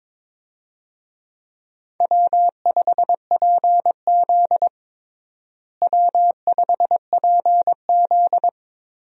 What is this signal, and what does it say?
Morse code, heard as a keyed tone, about 22 wpm, 700 Hz. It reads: W5PZ W5PZ
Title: USA Call Signs - 4 Characters - Repeated 3x - 22wpm